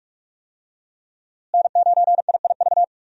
Morse code, heard as a keyed tone, about 42 wpm, 700 Hz. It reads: N9IIV
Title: USA Call Signs - 5 Characters - Repeated 2x - 45wpm